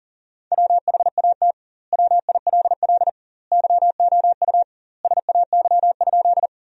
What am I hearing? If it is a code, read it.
Morse code, heard as a keyed tone, about 40 wpm, 700 Hz. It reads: WHAT WILL YOU SAY?
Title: 2x Word-Spacing - Sentences from Top 200 Words - 40wpm